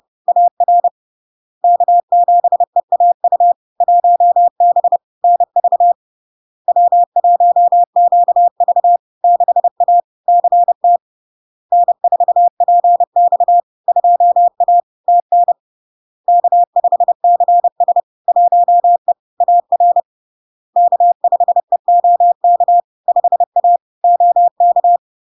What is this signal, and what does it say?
Morse code, heard as a keyed tone, about 30 wpm, 700 Hz. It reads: AR K7EAU 1B NV W1QV 6A CT N4PX 2A TN K5CS 1E AR K5EOK 5A OK
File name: LARK-FD-2022-Log-in-CW-30wpm-700-Hz